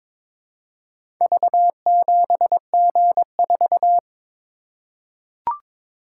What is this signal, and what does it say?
Morse code, heard as a keyed tone, about 22 wpm, 700 Hz. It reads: V7G4
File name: practice / 22wpm / Four Letter-Number Combinations - 22wpm